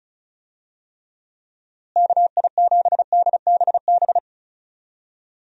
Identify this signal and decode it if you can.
Morse code, heard as a keyed tone, about 34 wpm, 700 Hz. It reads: KI7DBB
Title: USA Call Signs - 6 Characters - Repeated 2x - 35wpm